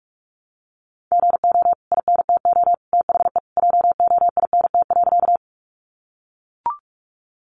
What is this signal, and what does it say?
Morse code, heard as a keyed tone, about 44 wpm, 700 Hz. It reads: GO INTO THE JOINT.
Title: Sentences from Top 1300 Words 45wpm